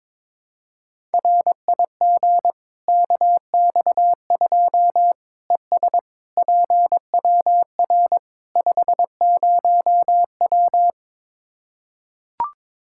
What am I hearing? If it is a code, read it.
Morse code, heard as a keyed tone, about 22 wpm, 700 Hz. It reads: RIG KX2 ES PWR 50W
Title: Practice QSO - Exchange 2 - Info - 22wpm